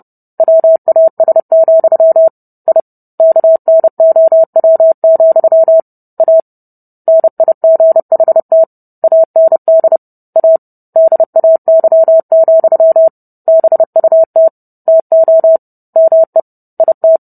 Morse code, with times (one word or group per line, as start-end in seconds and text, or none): 0.4-2.3 WAS,
2.7-2.8 I
3.2-5.8 KNOW,
6.2-6.4 A
7.1-8.6 NIGHT
9.0-10.0 AND
10.4-10.6 A
11.0-13.1 DAY,
13.5-14.5 BUT
14.9-15.6 TO
16.0-16.4 ME
16.8-17.2 IT